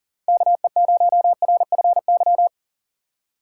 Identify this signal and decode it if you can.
Morse code, heard as a keyed tone, about 38 wpm, 690 Hz. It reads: KE0RFY
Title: USA Call Signs - 4x Character Spacing - 40wpm